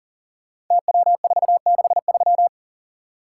Morse code, heard as a keyed tone, about 38 wpm, 700 Hz. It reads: TW463
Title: Code Groups - 5 Letter:Numbers - 40wpm